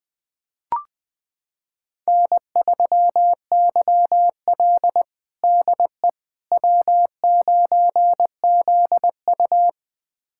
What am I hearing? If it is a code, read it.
Morse code, heard as a keyed tone, about 20 wpm, 700 Hz. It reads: N3YL DE W9ZU
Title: USA Callsign DE Callsign - 4 Characters - 20wpm